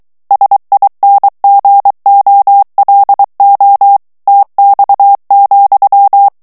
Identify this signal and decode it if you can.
Morse code, heard as a keyed tone, about 23 wpm, 800 Hz. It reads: SINGOLO TX,
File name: ITALIEN texte clair 2 vitesse 22 mots minute